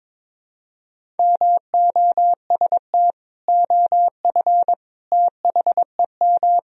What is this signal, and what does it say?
Morse code, heard as a keyed tone, about 22 wpm, 700 Hz. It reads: MOST OF THEM
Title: Sentences from Top 1000 Words - Review Part 1 - 22wpm